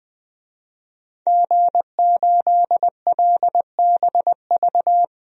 Morse code, heard as a keyed tone, about 20 wpm, 700 Hz. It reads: G8LBV
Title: DXCC-100 Call Signs - 5 Characters - 20wpm